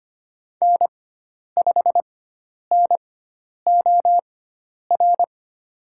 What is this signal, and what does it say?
Morse code, heard as a keyed tone, about 25 wpm, spaced as for 10 wpm, 700 Hz. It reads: N5NOR
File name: Lesson 16g - Letter c - Call Signs 25wpm